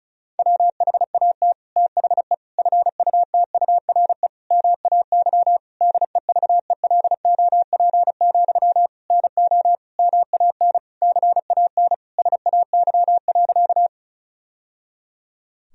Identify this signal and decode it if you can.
Morse code, heard as a keyed tone, about 34 wpm, 700 Hz. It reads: WHAT THE FUTURE MAY DEVELOP, NO MAN CAN SAY.
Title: Sentences from Top 1000 Words - Review Part 2 - 35wpm